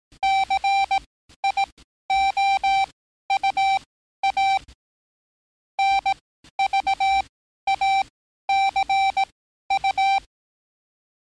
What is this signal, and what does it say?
Morse code, heard as a keyed tone, about 18 wpm, spaced as for 12 wpm, 780 Hz. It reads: CIOUA NVACU
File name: training / NAVPERS 11049to11060 / NAVPERS 11054 RA RB